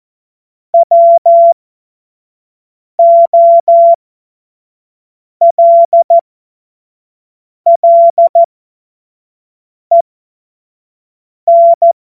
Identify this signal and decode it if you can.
Morse code, heard as a keyed tone, about 14 wpm, spaced as for 5 wpm, 680 Hz. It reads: WOLLEN
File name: Morsekurs / Witze / Witz 11. Januar 2022, 25 F70